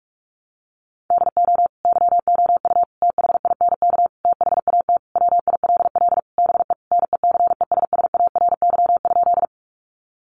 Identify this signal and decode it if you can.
Morse code, heard as a keyed tone, about 44 wpm, 700 Hz. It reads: DO YOU THINK THAT WILL BE NECESSARY?
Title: Sentences from Top 800 Words 45wpm